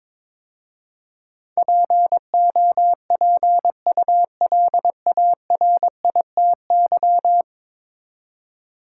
Morse code, heard as a keyed tone, about 22 wpm, 700 Hz. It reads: POPULARITY
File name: practice / 22wpm / 10 Letter Words - First 3 Letters Given - 22wpm